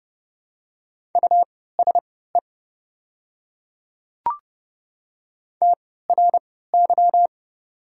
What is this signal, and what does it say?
Morse code, heard as a keyed tone, about 30 wpm, spaced as for 17 wpm, 700 Hz. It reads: USE TRY
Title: 3x Character Spacing - Top 300 Words - 30wpm